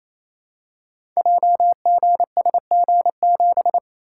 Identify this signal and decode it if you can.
Morse code, heard as a keyed tone, about 29 wpm, 700 Hz. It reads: JGSG7
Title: Code Groups - 5 Letter:Numbers - 28wpm